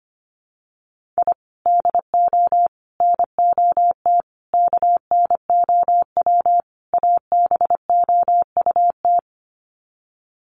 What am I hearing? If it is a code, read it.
Morse code, heard as a keyed tone, about 25 wpm, 700 Hz. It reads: I DO NOT KNOW ABOUT
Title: Sets of 5 Words 25wpm